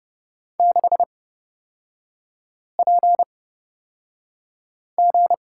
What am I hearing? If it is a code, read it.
Morse code, heard as a keyed tone, about 31 wpm, spaced as for 5 wpm, 690 Hz. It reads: 6PG
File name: Lesson 37e - Letter x - Three Characters 30wpm